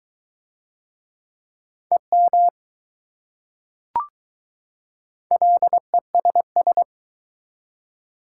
Morse code, heard as a keyed tone, about 20 wpm, 690 Hz. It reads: EM LESS